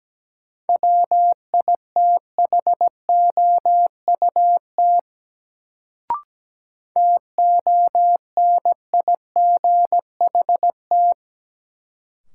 Morse code, heard as a keyed tone, about 17 wpm, 700 Hz. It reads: WITHOUT TONIGHT
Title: Sentences from Top 600 Words - New Word Drill - 17wpm